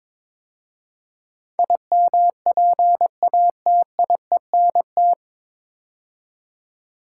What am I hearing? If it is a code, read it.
Morse code, heard as a keyed tone, about 22 wpm, 700 Hz. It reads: IMPATIENT
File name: 9 Letter Words - First 3 Letters Given - 22wpm